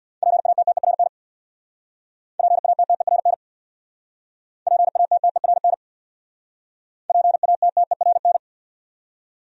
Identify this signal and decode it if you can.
Morse code, heard as a keyed tone, about 63 wpm, 700 Hz. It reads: PATTERN PATTERN PATTERN PATTERN